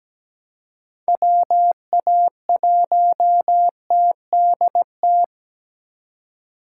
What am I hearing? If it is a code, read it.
Morse code, heard as a keyed tone, about 20 wpm, 700 Hz. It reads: WA1TDT
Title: Speed Racing - USA 2x3 Call Signs 17wpm